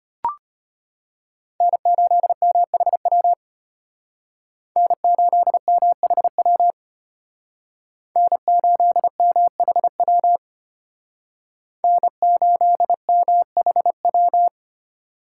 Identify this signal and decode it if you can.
Morse code, heard as a keyed tone, about 33 wpm, 700 Hz. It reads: N8MHW N8MHW N8MHW N8MHW